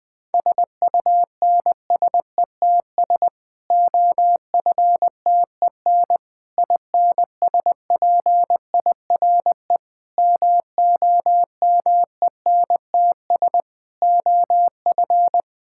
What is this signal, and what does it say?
Morse code, heard as a keyed tone, about 20 wpm, 700 Hz. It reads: SUNSETS OFTEN INSPIRE MOMENTS OF
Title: Difficult Sentences with Hints - 20wpm